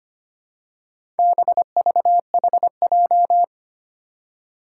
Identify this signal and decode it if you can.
Morse code, heard as a keyed tone, about 25 wpm, 700 Hz. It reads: BVHJ
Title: Four Challenging Letter-Number Combinations 25wpm